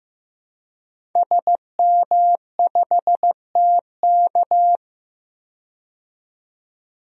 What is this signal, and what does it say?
Morse code, heard as a keyed tone, about 15 wpm, 700 Hz. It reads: SM5TK
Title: DXCC-100 Call Signs - 5 Characters - 15wpm